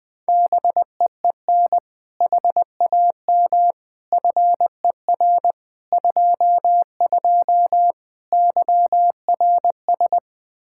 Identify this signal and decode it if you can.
Morse code, heard as a keyed tone, about 21 wpm, 700 Hz. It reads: BEEN HAM FER 22 YRS
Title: Practice QSO - Exchange 3 - More Info - 20wpm